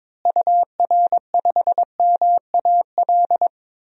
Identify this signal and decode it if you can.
Morse code, heard as a keyed tone, about 22 wpm, 700 Hz. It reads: UR5MAL
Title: DXCC-100 Call Signs - 6 Characters - Repeated 3x - 22wpm